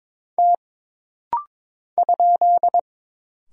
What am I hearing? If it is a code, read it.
Morse code, heard as a keyed tone, about 22 wpm, 700 Hz. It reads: T ?
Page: This rapid-fire timing is geared to push your ICR (Instant Character Recognition) skills.